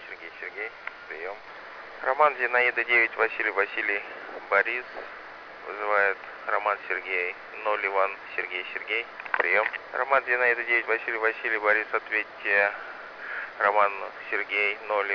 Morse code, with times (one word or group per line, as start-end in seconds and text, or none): none